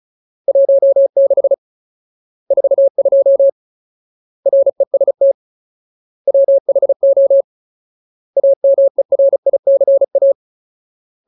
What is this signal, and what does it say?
Morse code, heard as a keyed tone, about 34 wpm, 550 Hz. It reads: 16 42 REST WHO AMERICA
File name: Ultimate Word List / 35WPM Ultimate